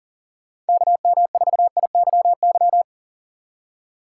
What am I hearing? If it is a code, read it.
Morse code, heard as a keyed tone, about 49 wpm, 700 Hz. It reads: KM4IYY
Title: Speed Racing - USA Call Signs 40wpm